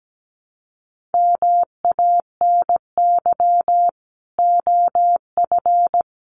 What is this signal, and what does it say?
Morse code, heard as a keyed tone, about 17 wpm, 700 Hz. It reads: MANY OF
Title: Sets of 2 Words 17wpm